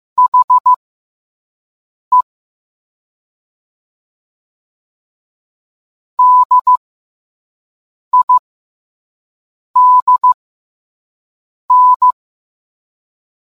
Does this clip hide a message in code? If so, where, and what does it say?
Yes: Morse code from 0.2 to 12.1 s, heard as a keyed tone, about 15 wpm, spaced as for 5 wpm, 1000 Hz. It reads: HE DIDN